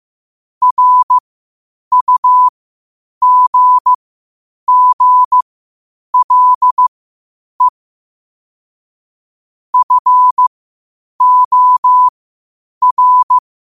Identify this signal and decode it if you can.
Morse code, heard as a keyed tone, about 15 wpm, spaced as for 8 wpm, 1000 Hz. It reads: RUGGLE FOR
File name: qotdfiles / 8wpm